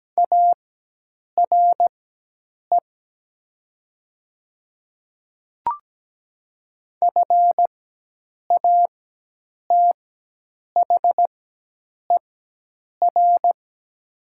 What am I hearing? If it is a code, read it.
Morse code, heard as a keyed tone, about 17 wpm, spaced as for 8 wpm, 700 Hz. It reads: ARE FATHER
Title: 4x Character Spacing - Top 200 Words - 17wpm